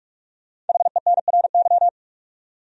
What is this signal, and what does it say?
Morse code, heard as a keyed tone, about 46 wpm, 700 Hz. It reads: HENRY
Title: USA Common Names - 45wpm